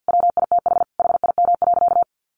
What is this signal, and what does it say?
Morse code, heard as a keyed tone, about 50 wpm, 700 Hz. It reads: WITH HIM.
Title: Sentences from Top 1500 Words 50wpm